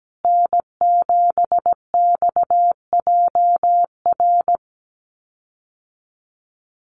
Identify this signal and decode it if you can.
Morse code, heard as a keyed tone, about 17 wpm, 700 Hz. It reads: N7XJR
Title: USA Call Signs Repeated 3x 17wpm